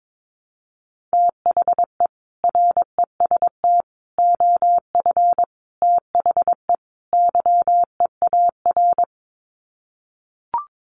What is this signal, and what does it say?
Morse code, heard as a keyed tone, about 22 wpm, 700 Hz. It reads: THE REST OF THE YEAR
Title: Sets of 5 Words Encore 22wpm